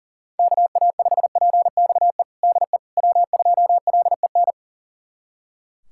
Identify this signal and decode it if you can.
Morse code, heard as a keyed tone, about 40 wpm, 700 Hz. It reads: KA5PXE DE W2LEN
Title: USA Callsign DE Callsign 40wpm